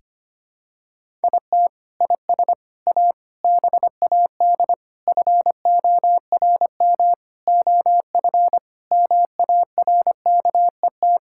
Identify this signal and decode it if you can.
Morse code, heard as a keyed tone, about 25 wpm, 700 Hz. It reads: IT IS A BAD FORM OF MARKET
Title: Sentences from Top 1000 Words - Review Part 5 - 25wpm